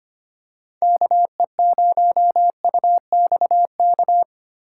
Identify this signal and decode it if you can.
Morse code, heard as a keyed tone, about 25 wpm, 700 Hz. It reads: KE0UXK
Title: USA Call Signs - 8x Character Spacing - 25wpm